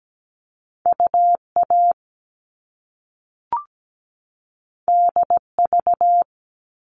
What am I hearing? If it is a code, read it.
Morse code, heard as a keyed tone, about 17 wpm, 700 Hz. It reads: UA DV
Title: All Cut Numbers - 2-Digit 17wpm